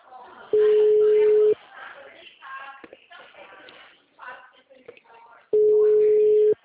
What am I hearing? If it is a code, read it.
Ringback tone (Europe, 425 Hz), with speech.
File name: out- -2010- - - .69936